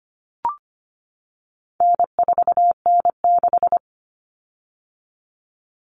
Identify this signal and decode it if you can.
Morse code, heard as a keyed tone, about 25 wpm, 700 Hz. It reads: N4N6
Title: Common Cut Numbers - 4-Digit 25wpm